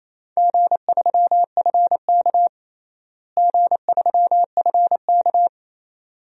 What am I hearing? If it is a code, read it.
Morse code, heard as a keyed tone, about 28 wpm, 700 Hz. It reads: G3FK G3FK
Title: DXCC-100 Call Signs - 4 Characters - Repeated 2x - 28wpm